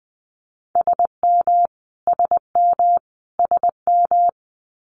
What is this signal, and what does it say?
Morse code, heard as a keyed tone, about 20 wpm, 700 Hz. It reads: SM SM SM
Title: Multi-letter Phonemes Repeated 3x 20wpm